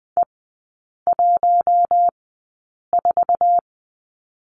Morse code, heard as a keyed tone, about 20 wpm, spaced as for 8 wpm, 700 Hz. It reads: E14
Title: 053 Lesson 10e - Letter r - Three Characters 20wpm